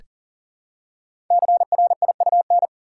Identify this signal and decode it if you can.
Morse code, heard as a keyed tone, about 42 wpm, 700 Hz. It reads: CRIUN
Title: Code Groups - 5 Letters - 40wpm